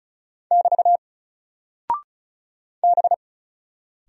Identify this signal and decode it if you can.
Morse code, heard as a keyed tone, about 34 wpm, 700 Hz. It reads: <BT> B
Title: Single Character (Letter:Number:Punctuation:Prosign) Alt 35wpm